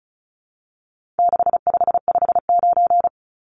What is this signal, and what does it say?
Morse code, heard as a keyed tone, about 34 wpm, 700 Hz. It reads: 6559
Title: Numbers - 4-Digit 35wpm